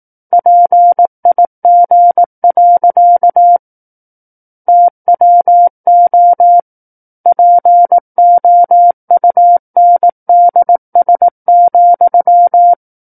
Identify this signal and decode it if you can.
Morse code, heard as a keyed tone, about 18 wpm, 700 Hz. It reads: PIG. TWO POUNDS,
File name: 18 WPM